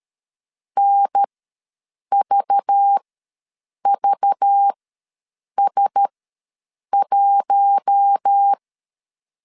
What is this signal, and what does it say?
Morse code, heard as a keyed tone, about 13 wpm, spaced as for 7 wpm, 800 Hz. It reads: NVVS1
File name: LEZIONI / 5 LBVC / 5 LBV ws 12wpm 3015